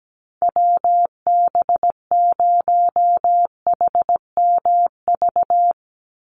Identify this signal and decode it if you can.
Morse code, heard as a keyed tone, about 17 wpm, 700 Hz. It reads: WB0HMV
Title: USA Call Signs 17wpm